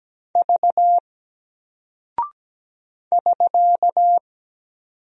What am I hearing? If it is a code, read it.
Morse code, heard as a keyed tone, about 17 wpm, 700 Hz. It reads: V <SK>
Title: Single Character (Letter:Number:Punctuation:Prosign) Alt 17wpm